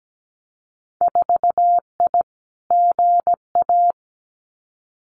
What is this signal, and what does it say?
Morse code, heard as a keyed tone, about 17 wpm, 700 Hz. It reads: 4I GA
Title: Winter Field Day Exchanges Encore 17wpm